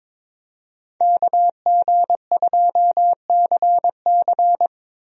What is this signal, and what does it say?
Morse code, heard as a keyed tone, about 22 wpm, 700 Hz. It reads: KG2CC
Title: USA Call Signs - 5 Characters - Repeated 2x - 22wpm